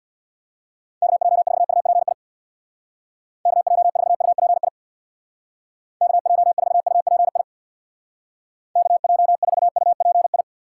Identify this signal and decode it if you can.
Morse code, heard as a keyed tone, about 64 wpm, 700 Hz. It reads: KJ4UPI KJ4UPI KJ4UPI KJ4UPI